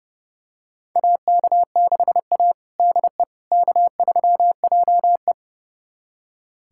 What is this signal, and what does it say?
Morse code, heard as a keyed tone, about 30 wpm, 700 Hz. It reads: AK6A DE K3JE
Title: USA Callsign DE Callsign - 4 Characters - 30wpm